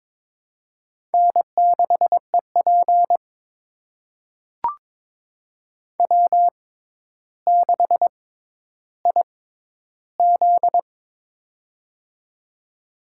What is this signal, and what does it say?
Morse code, heard as a keyed tone, about 22 wpm, spaced as for 11 wpm, 700 Hz. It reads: N6EP W6IZ
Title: USA Call Signs - 4 Characters - 6x Character Spacing - 22wpm